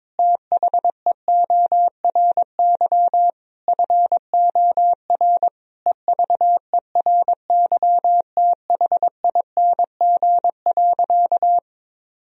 Morse code, heard as a keyed tone, about 22 wpm, 700 Hz. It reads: THEORY FOR EVERYTHING.
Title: Sentences from Top 1000 Words - Review Part 3 - 22wpm